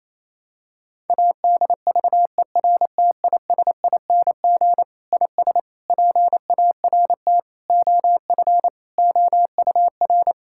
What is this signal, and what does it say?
Morse code, heard as a keyed tone, about 28 wpm, 700 Hz. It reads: ADVERTISING IS PART OF OUR
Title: Sentences from Top 1000 Words - Review Part 6 - 28wpm